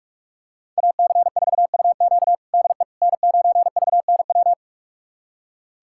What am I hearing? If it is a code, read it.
Morse code, heard as a keyed tone, about 45 wpm, 700 Hz. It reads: AC4UQ DE N9VNW
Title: USA Callsign DE Callsign - 5 Characters - 45wpm